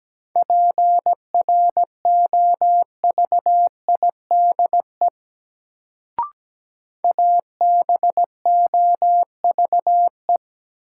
Morse code, heard as a keyed tone, about 17 wpm, 700 Hz. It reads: PROVIDE ABOVE
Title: Sentences from Top 1000 Words - Review Part 2 - New Word Drill - 17wpm